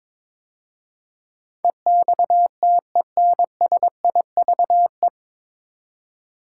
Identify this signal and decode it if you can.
Morse code, heard as a keyed tone, about 22 wpm, 700 Hz. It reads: EXTENSIVE
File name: 9 Letter Words - First 3 Letters Given - 22wpm